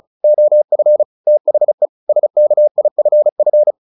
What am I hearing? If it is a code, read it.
Morse code, heard as a keyed tone, about 34 wpm, 600 Hz. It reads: OF THE SKIFF
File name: the old man and the sea / the old man and the sea 1 35wpm